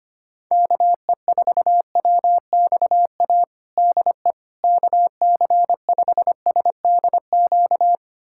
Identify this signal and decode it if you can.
Morse code, heard as a keyed tone, about 25 wpm, 700 Hz. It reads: KE4WXA DE KC5SDQ
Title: USA Callsign DE Callsign - 6 Characters - 25wpm